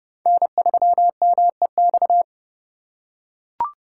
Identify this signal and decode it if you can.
Morse code, heard as a keyed tone, about 30 wpm, 700 Hz. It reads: N3MEX
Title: USA Call Signs - 5 Characters - 4x Character Spacing - 30wpm